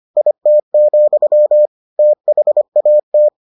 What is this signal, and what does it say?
Morse code, heard as a keyed tone, about 25 wpm, 600 Hz. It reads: IT, THAT